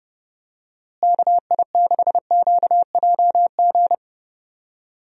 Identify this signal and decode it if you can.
Morse code, heard as a keyed tone, about 30 wpm, 700 Hz. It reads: KI6QJG
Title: USA Call Signs - 6 Characters - Repeated 2x - 30wpm